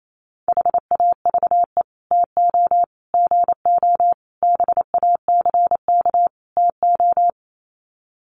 Morse code, heard as a keyed tone, about 28 wpm, 700 Hz. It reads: HAVE TO GO BACK TO
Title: Sets of 5 Words Encore2 28wpm